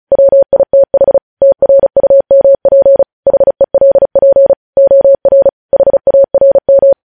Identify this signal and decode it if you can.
Morse code, heard as a keyed tone, about 36 wpm, 550 Hz. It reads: WITH TRUMP HELP OR HARM
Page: news headlines in morse code at 35 wpm